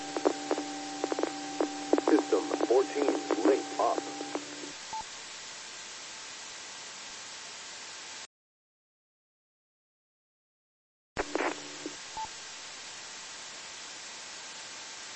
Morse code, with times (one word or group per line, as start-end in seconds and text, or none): none